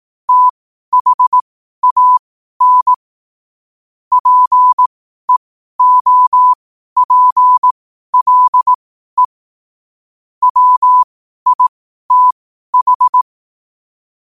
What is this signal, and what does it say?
Morse code, heard as a keyed tone, about 18 wpm, spaced as for 12 wpm, 1000 Hz. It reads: THAN PEOPLE WITH
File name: qotdfiles / 12wpm